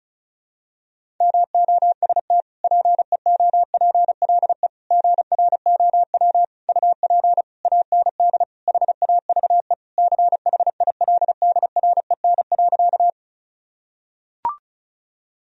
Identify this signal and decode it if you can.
Morse code, heard as a keyed tone, about 35 wpm, 700 Hz. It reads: MOST PEOPLE GROW UP AND HAVE CHILDREN.
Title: Sentences from Top 1000 Words - Review Part 1 - 35wpm